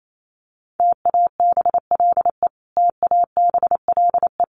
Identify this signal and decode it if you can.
Morse code, heard as a keyed tone, about 28 wpm, 700 Hz. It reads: TABLE TABLE
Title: Top 500 Words Repeated 28wpm